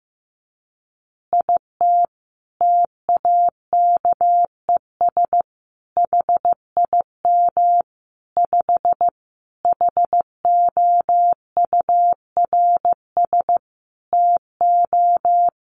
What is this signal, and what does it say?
Morse code, heard as a keyed tone, about 15 wpm, 700 Hz. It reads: IT TAKES HIM 5 HOURS TO
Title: Additional Easy Phrases 15wpm